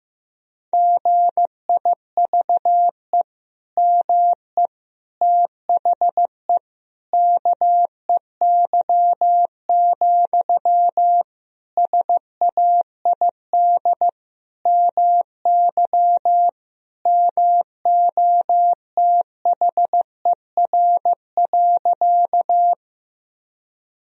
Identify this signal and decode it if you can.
Morse code, heard as a keyed tone, about 16 wpm, 700 Hz. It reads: GIVE ME THE KEY, SAID MY MOTHER.
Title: Sentences from Top 1000 Words - Review Part 4 - 15wpm